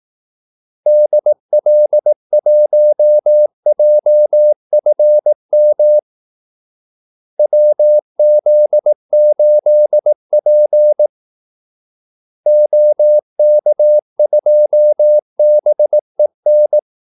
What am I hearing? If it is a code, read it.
Morse code, heard as a keyed tone, about 18 wpm, 600 Hz. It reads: DL1JFM WZ8P OK2BEN
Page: Kaldesignaler 18 wpm